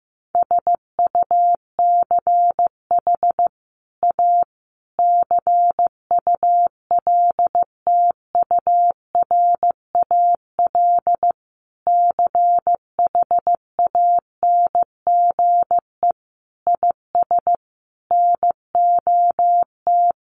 Morse code, heard as a keyed tone, about 15 wpm, 700 Hz. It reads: SUCH A CULTURAL CHANGE IS NOT
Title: Sentences from Top 1500 Words 15wpm